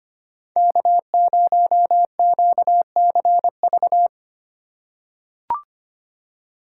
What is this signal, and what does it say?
Morse code, heard as a keyed tone, about 25 wpm, 700 Hz. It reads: K0QCV
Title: USA Call Signs - 5 Characters - 2x Character Spacing - 25wpm